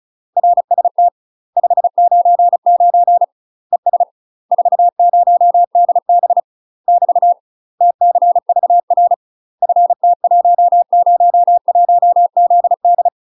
Morse code, heard as a keyed tone, about 34 wpm, 700 Hz. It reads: RST 599 ES 40DB <BT> TCVR FT101ZD
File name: UK MASTER QSOs TRAINING / 35WPM / UK MASTER QSOs 0000